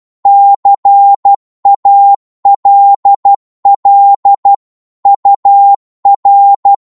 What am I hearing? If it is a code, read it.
Morse code, heard as a keyed tone, about 12 wpm, 800 Hz. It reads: CALL UR
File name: Morse / qso-12wpm